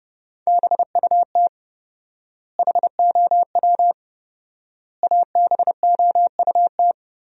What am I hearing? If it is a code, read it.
Morse code, heard as a keyed tone, about 32 wpm, 700 Hz. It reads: BUT HOW ABOUT
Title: 4x Word-Spacing - Sentences from Top 100 Words - 30wpm